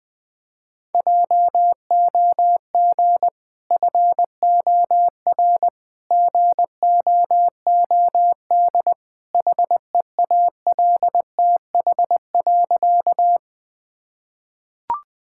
Morse code, heard as a keyed tone, about 20 wpm, 700 Hz. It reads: JOG FOR GOOD HEALTH.
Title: Easy Sentences with Hints - 20wpm